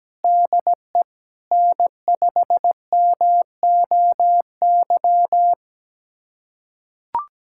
Morse code, heard as a keyed tone, about 17 wpm, 700 Hz. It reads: DE N5MOY
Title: USA Callsign DE Callsign - 5 Characters - 17wpm